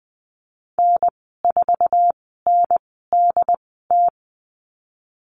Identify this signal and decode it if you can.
Morse code, heard as a keyed tone, about 20 wpm, spaced as for 15 wpm, 700 Hz. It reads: N4NDT
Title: 076 Lesson 13g - Letter L - Call Signs 20wpm